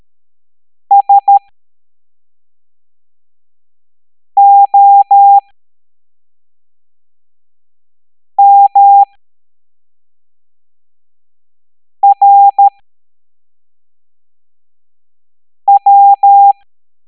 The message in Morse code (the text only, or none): SOMRW